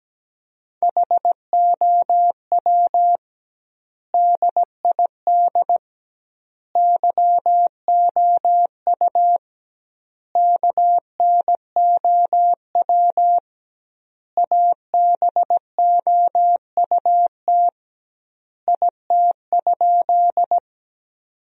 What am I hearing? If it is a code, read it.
Morse code, heard as a keyed tone, about 17 wpm, 700 Hz. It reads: HOW DID YOU KNOW ABOUT IT?
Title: 2x Word-Spacing - Sentences from Top 100 Words - 17wpm